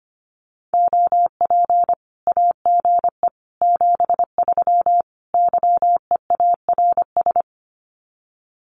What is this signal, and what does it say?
Morse code, heard as a keyed tone, about 25 wpm, 700 Hz. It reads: OP AGE 73 YEARS
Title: Common QSO Phrases 25wpm